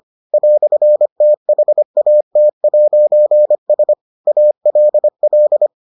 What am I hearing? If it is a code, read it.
Morse code, heard as a keyed tone, about 25 wpm, 600 Hz. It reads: "THAT'S ALL